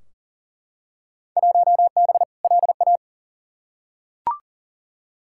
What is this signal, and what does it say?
Morse code, heard as a keyed tone, about 40 wpm, 700 Hz. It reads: 1B LA
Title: Field Day Exchanges Encore - 40wpm